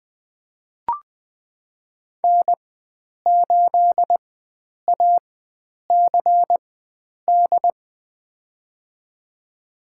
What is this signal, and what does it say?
Morse code, heard as a keyed tone, about 21 wpm, spaced as for 9 wpm, 700 Hz. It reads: N8ACD
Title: USA Call Signs - 4x Character Spacing - 20wpm